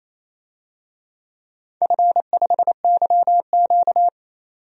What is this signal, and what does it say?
Morse code, heard as a keyed tone, about 28 wpm, 700 Hz. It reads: F5YQ
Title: DXCC-100 Call Signs - 4 Characters - 28wpm